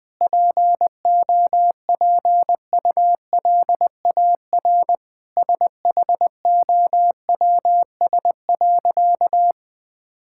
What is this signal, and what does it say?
Morse code, heard as a keyed tone, about 20 wpm, 700 Hz. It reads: POPULAR SHOWS.
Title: Sentences with 5-Word Phrases - Spoken Beginning - 20wpm